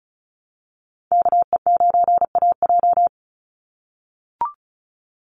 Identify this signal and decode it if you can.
Morse code, heard as a keyed tone, about 34 wpm, 700 Hz. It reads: KE9AJ
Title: NA SOTA Chaser Callsigns 35wpm